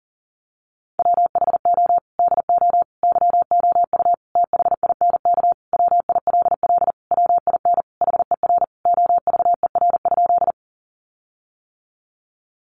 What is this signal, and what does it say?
Morse code, heard as a keyed tone, about 40 wpm, 700 Hz. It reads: WHO DO YOU THINK WILL WIN HER OVER?
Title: Sentences from Top 1500 Words 40wpm